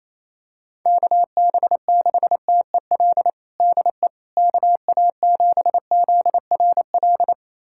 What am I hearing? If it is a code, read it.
Morse code, heard as a keyed tone, about 28 wpm, 700 Hz. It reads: KB6TEL DE KA7ZRL
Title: USA Callsign DE Callsign 28wpm